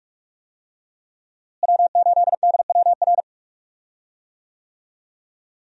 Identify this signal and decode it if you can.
Morse code, heard as a keyed tone, about 46 wpm, 700 Hz. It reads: W8DWR
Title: USA Call Signs - 5 Characters - Repeated 3x - 45wpm